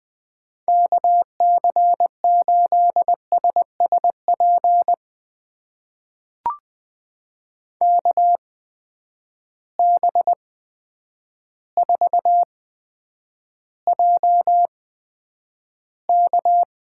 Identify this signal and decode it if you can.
Morse code, heard as a keyed tone, about 20 wpm, spaced as for 9 wpm, 700 Hz. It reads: KC8SSP KB4JK
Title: USA Call Signs - 6 Characters - 8x Character Spacing - 20wpm